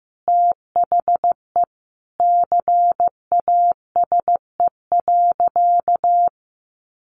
Morse code, heard as a keyed tone, about 15 wpm, 700 Hz. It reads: THE CASE.
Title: Sentences from Top 1400 Words 15wpm